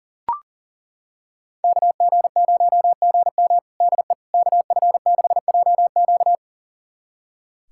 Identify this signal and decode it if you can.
Morse code, heard as a keyed tone, about 40 wpm, 700 Hz. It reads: KG0GM DE KF6JQ
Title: USA Callsign DE Callsign - 5 Characters - 40wpm